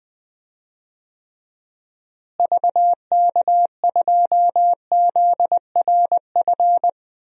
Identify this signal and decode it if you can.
Morse code, heard as a keyed tone, about 20 wpm, 700 Hz. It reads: VK2ZRF
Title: DXCC-100 Call Signs - 6 Characters - Repeated 2x - 20wpm